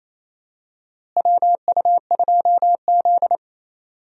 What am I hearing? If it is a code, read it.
Morse code, heard as a keyed tone, about 28 wpm, 700 Hz. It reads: WU2Z
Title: USA Call Signs - 4 Characters - Repeated 3x - 28wpm